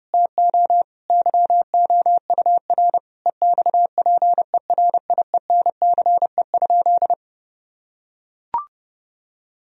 Morse code, heard as a keyed tone, about 30 wpm, 700 Hz. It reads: TO YOUR EXPERIENCE?
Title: Sentences from Top 1000 Words - Review Part 4 - 30wpm